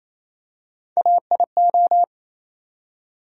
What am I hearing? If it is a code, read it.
Morse code, heard as a keyed tone, about 28 wpm, 700 Hz. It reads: AIO
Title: USA Call Sign Postfixes - 3 Letters 28wpm